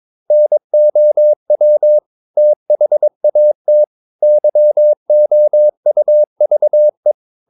Morse code, heard as a keyed tone, about 22 wpm, 600 Hz. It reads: NOW THAT YOUVE
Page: Romanen Never af Ken Follet på engelsk.